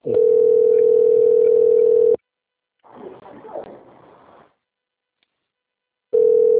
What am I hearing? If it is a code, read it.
Ringback tone (North America), 440 + 480 Hz.
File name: rg-1600-unknown- - - .983